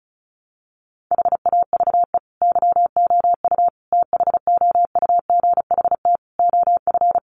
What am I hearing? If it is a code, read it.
Morse code, heard as a keyed tone, about 34 wpm, 700 Hz. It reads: HAVE YOU THOUGHT OF
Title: Sentences from Top 700 Words 35wpm